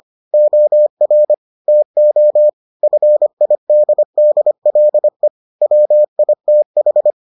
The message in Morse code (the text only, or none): OR TO FIDDLE WITH